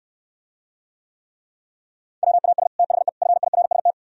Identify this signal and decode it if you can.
Morse code, heard as a keyed tone, about 68 wpm, 700 Hz. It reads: WAS THE FIRST